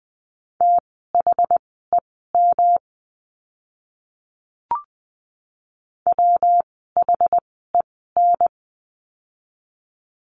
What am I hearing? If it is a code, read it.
Morse code, heard as a keyed tone, about 20 wpm, spaced as for 14 wpm, 700 Hz. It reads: THEM WHEN
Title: 192 Lesson 31f - Letter v - Words 20wpm